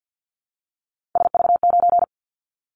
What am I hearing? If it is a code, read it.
Morse code, heard as a keyed tone, about 50 wpm, 700 Hz. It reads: SV9
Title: Callsign Prefixes for Top 100 DXCC - 50wpm